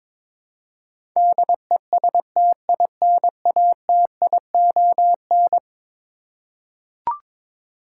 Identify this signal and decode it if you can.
Morse code, heard as a keyed tone, about 22 wpm, 700 Hz. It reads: DESTINATION
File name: practice / 22wpm / Eleven Letter Words - 22wpm